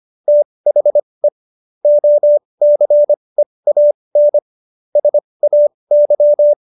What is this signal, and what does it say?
Morse code, heard as a keyed tone, about 25 wpm, spaced as for 20 wpm, 590 Hz. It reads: THE OCEAN SAY
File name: Jokes-25-char-20-eff